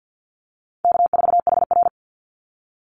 Morse code, heard as a keyed tone, about 49 wpm, 700 Hz. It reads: K4HR
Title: Call Signs Repeated 3x - 50wpm